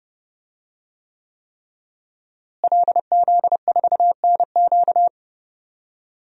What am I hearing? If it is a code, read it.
Morse code, heard as a keyed tone, about 30 wpm, 700 Hz. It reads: LZ4NQ
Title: DXCC-100 Call Signs - 5 Characters - Repeated 3x - 30wpm